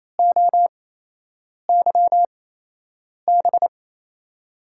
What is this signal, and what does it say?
Morse code, heard as a keyed tone, about 28 wpm, spaced as for 8 wpm, 700 Hz. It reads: OYB
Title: USA Call Signs - 5 Characters - 8x Character Spacing - 28wpm